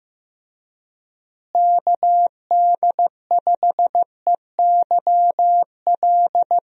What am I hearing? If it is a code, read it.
Morse code, heard as a keyed tone, about 15 wpm, 700 Hz. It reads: KD5EYL
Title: USA Call Signs - 6 Characters - Repeated 3x - 15wpm